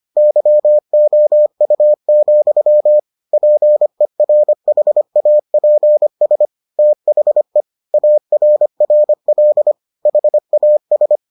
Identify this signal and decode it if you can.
Morse code, heard as a keyed tone, about 25 wpm, 600 Hz. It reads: YOU, PERHAPS THE ARRL HAS